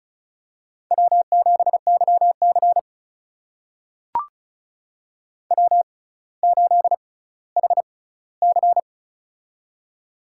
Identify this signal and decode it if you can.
Morse code, heard as a keyed tone, about 36 wpm, spaced as for 18 wpm, 700 Hz. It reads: W7YC W8HC
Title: USA Call Signs - 4 Characters - 6x Character Spacing - 35wpm